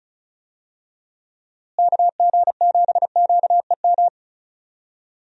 Morse code, heard as a keyed tone, about 36 wpm, 700 Hz. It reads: KG7QEM
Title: USA Call Signs - 6 Characters - Repeated 3x - 35wpm